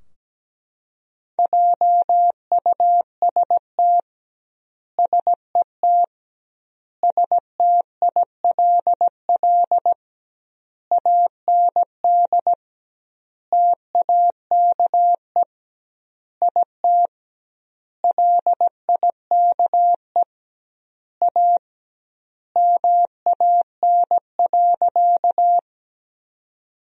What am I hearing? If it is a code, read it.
Morse code, heard as a keyed tone, about 17 wpm, 700 Hz. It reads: JUST SET STILL AND TAKE IT LIKE A MAN.
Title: 2x Word-Spacing - Sentences from Top 300 Words - 17wpm